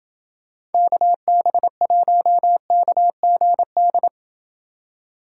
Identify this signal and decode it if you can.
Morse code, heard as a keyed tone, about 28 wpm, 700 Hz. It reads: KB1KGD